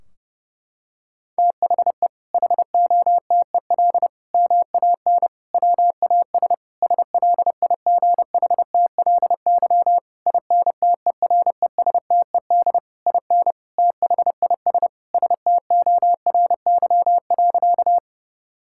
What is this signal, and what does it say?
Morse code, heard as a keyed tone, about 30 wpm, 700 Hz. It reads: THE HOTEL MAN WAS SLIGHTLY INTERESTED IN THIS STORY.
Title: Sentences from Top 1600 Words 30wpm